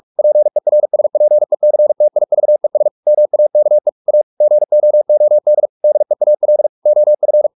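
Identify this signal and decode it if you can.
Morse code, heard as a keyed tone, about 44 wpm, 600 Hz. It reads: PERSPECTIVES MAKE A GOOD DEAL OF
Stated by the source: Morse practice at 45 wpm, the novel Never by Ken Follett in English, from CW med Gnister.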